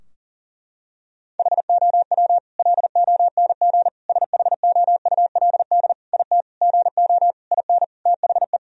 Morse code, heard as a keyed tone, about 40 wpm, 700 Hz. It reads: HOW LONG SHOULD IT GO IN THE
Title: Sentences from Top 1000 Words - Review Part 4 - 40wpm